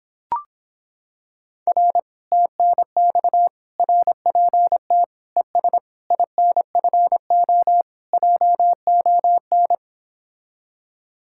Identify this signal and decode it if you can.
Morse code, heard as a keyed tone, about 26 wpm, 700 Hz. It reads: R TNX RPT ES INFO JON